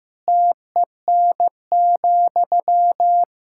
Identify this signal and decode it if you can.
Morse code, heard as a keyed tone, about 15 wpm, 700 Hz. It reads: TEN,
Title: Sentences from Top 1000 Words - Review Part 6 - 15wpm